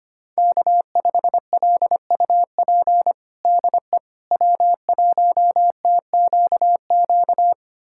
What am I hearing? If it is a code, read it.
Morse code, heard as a keyed tone, about 25 wpm, 700 Hz. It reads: K5LUP DE W1TQQ
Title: USA Callsign DE Callsign - 5 Characters - 25wpm